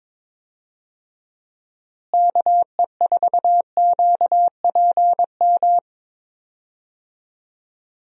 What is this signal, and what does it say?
Morse code, heard as a keyed tone, about 22 wpm, 700 Hz. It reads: KE4QPM
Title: USA Call Signs - 6 Characters - Repeated 3x - 22wpm